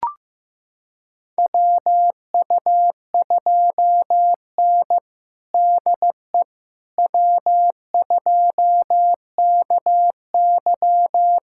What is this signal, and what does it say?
Morse code, heard as a keyed tone, about 15 wpm, 700 Hz. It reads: WU2N DE W2KY
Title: USA Callsign DE Callsign - 4 Characters - 15wpm